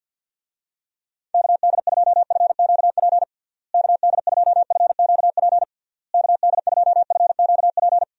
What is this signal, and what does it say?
Morse code, heard as a keyed tone, about 52 wpm, 700 Hz. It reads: KD2FQP KD2FQP KD2FQP
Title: USA Call Signs - 6 Characters - Repeated 3x - 50wpm